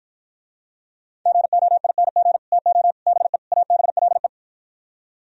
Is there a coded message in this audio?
Yes: Morse code from 1.3 to 4.3 s, heard as a keyed tone, about 52 wpm, 700 Hz. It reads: GOING TO BE ABLE